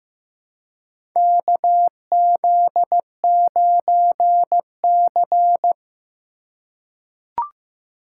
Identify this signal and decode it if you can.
Morse code, heard as a keyed tone, about 15 wpm, 700 Hz. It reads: KZ9C
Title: USA Call Signs - 4 Characters - 2x Character Spacing - 15wpm